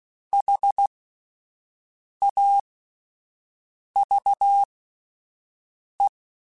Morse code, heard as a keyed tone, about 16 wpm, spaced as for 5 wpm, 800 Hz. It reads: HAVE
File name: Spaced / Chuckles Spaced-mp3s / SoftWords sp @10